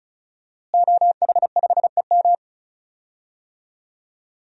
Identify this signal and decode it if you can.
Morse code, heard as a keyed tone, about 34 wpm, 700 Hz. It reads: OH5EM
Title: DXCC-100 Call Signs - 5 Characters - Repeated 3x - 35wpm